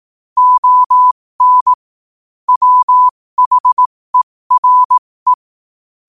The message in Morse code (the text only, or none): ON WHERE